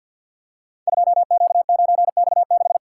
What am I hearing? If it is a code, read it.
Morse code, heard as a keyed tone, about 46 wpm, 700 Hz. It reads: 2Q9X6
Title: Code Groups - 5 Letter:Numbers - 50wpm